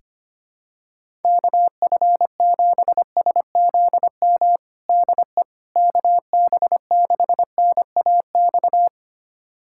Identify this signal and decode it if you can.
Morse code, heard as a keyed tone, about 25 wpm, 700 Hz. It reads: KF7SZM DE KB6NAX
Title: USA Callsign DE Callsign - 6 Characters - 25wpm